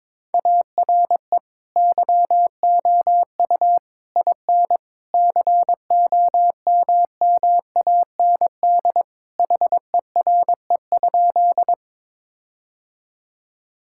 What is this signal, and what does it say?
Morse code, heard as a keyed tone, about 22 wpm, 700 Hz. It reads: ARE YOU IN COMMAND HERE?
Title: Sentences from Top 1600 Words 22wpm